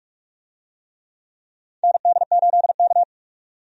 Morse code, heard as a keyed tone, about 46 wpm, 700 Hz. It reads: ND8K
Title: USA Call Signs - 4 Characters - 45wpm